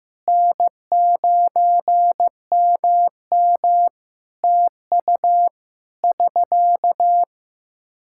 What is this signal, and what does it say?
Morse code, heard as a keyed tone, about 15 wpm, 700 Hz. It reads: N9MM TU <SK>
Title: Practice QSO - Exchange 4 - Ending - 15wpm